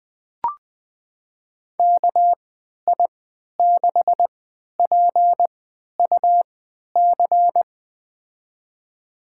Morse code, heard as a keyed tone, about 20 wpm, spaced as for 11 wpm, 700 Hz. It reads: KI6PUC
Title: USA Call Signs - 3x Character Spacing - 20wpm